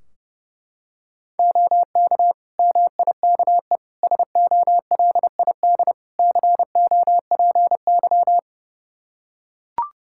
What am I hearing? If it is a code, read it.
Morse code, heard as a keyed tone, about 32 wpm, 700 Hz. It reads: OK MIKE SOLID COPY
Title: Practice QSO - Exchange 3 - More Info - 30wpm